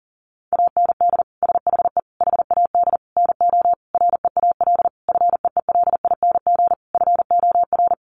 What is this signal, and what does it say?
Morse code, heard as a keyed tone, about 40 wpm, 700 Hz. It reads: AND SHE HAD NO REAL FEELING FOR
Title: Sentences from Top 1200 Words 40wpm